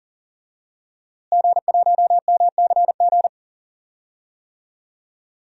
Morse code, heard as a keyed tone, about 40 wpm, 700 Hz. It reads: G1MCG
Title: DXCC-100 Call Signs - 5 Characters - Repeated 3x - 40wpm